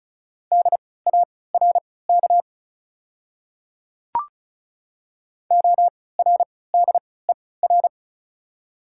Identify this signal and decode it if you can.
Morse code, heard as a keyed tone, about 36 wpm, spaced as for 20 wpm, 690 Hz. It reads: DARK ORDER
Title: 3x Character Spacing - Top 500 Words - 35wpm